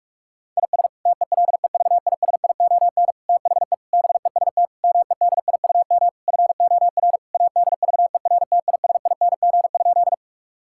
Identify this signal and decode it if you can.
Morse code, heard as a keyed tone, about 46 wpm, 700 Hz. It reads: IS TELEVISION THE BEST MEDIUM FOR ADVERTISING?
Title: Sentences from Top 1000 Words - Review Part 6 - 45wpm